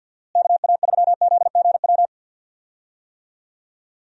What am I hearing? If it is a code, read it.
Morse code, heard as a keyed tone, about 46 wpm, 700 Hz. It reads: KA3ZGW
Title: USA Call Signs - 6 Characters - Repeated 3x - 50wpm